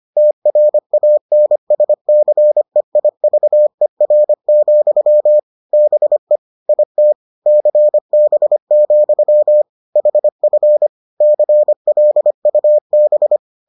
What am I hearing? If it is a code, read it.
Morse code, heard as a keyed tone, about 25 wpm, 600 Hz. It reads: TRANSCEIVER, BE IT CB, HF CLUB